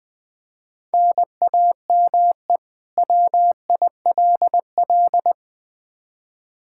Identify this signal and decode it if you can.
Morse code, heard as a keyed tone, about 20 wpm, 700 Hz. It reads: NAME WILL
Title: K1USN SST QSO Practice 20wpm